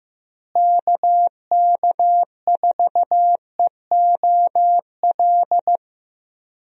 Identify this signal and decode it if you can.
Morse code, heard as a keyed tone, about 15 wpm, 700 Hz. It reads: KK4EOL
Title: USA Call Signs - 6 Characters - Repeated 3x - 15wpm